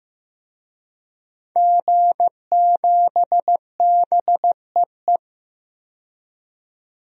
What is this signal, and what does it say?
Morse code, heard as a keyed tone, about 15 wpm, 700 Hz. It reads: G7BEE
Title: DXCC-100 Call Signs - 5 Characters - 15wpm